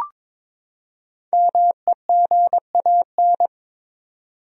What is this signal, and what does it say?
Morse code, heard as a keyed tone, about 22 wpm, 700 Hz. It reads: MEGAN
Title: USA Common Names - 22wpm